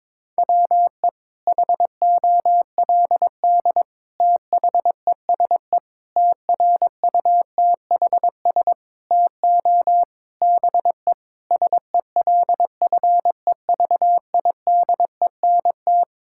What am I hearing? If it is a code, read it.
Morse code, heard as a keyed tone, about 22 wpm, 700 Hz. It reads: WE HOLD THESE TRUTHS TO BE SELFEVIDENT
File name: Sets of 9 Words - 22wpm